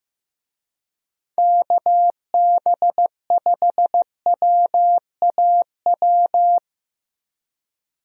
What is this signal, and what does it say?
Morse code, heard as a keyed tone, about 15 wpm, 700 Hz. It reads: KB5WAW
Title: USA Call Signs - 6 Characters - Repeated 3x - 15wpm